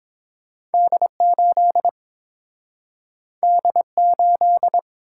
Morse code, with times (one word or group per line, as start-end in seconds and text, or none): 0.7-1.1 D
1.2-1.9 8
3.4-3.8 D
4.0-4.8 8